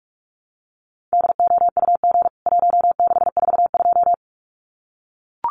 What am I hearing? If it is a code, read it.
Morse code, heard as a keyed tone, about 44 wpm, 700 Hz. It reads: DOUG 1642
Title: CWT Exchange 45wpm